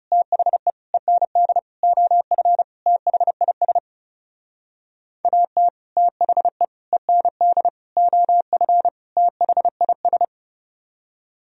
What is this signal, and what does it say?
Morse code, heard as a keyed tone, about 40 wpm, 700 Hz. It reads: THE END OF THIS AT THE END OF THIS